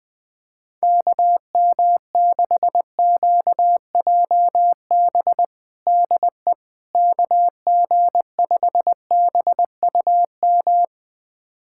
Morse code, heard as a keyed tone, about 20 wpm, 700 Hz. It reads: KM6QJB DE KG5BUM
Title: USA Callsign DE Callsign - 6 Characters - 20wpm